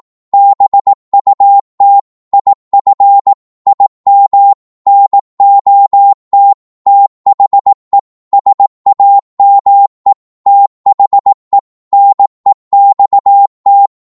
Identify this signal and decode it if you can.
Morse code, heard as a keyed tone, about 18 wpm, 800 Hz. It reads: BUT IF IM NOT THE SAME THE NEXT
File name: Morse / psg-18wpm